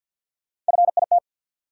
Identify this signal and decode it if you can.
Morse code, heard as a keyed tone, about 48 wpm, 700 Hz. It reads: FIT
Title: Three Letter Words - 50wpm